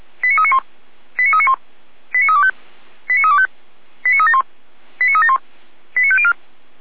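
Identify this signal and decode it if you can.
Five-tone selective call (ZVEI), heard as quick runs of five tones, tones of about 70 ms. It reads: 89391 89391 89325 89325 89371 89371 89504
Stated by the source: ZVEI five-tone selective call 89042 received by SDR.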